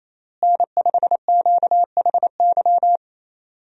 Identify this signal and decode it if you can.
Morse code, heard as a keyed tone, about 28 wpm, 700 Hz. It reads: N5QHY
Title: USA Call Signs - 5 Characters - Repeated 3x - 28wpm